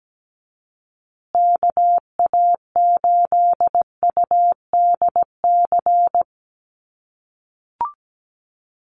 Morse code, heard as a keyed tone, about 17 wpm, 700 Hz. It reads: KA8UDC
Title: USA Call Signs Repeated 2x 17wpm